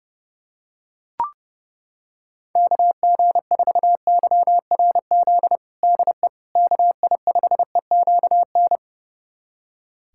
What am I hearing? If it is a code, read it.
Morse code, heard as a keyed tone, about 30 wpm, 700 Hz. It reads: KG4YRZ DE KI5EQN
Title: USA Callsign DE Callsign - 6 Characters - 30wpm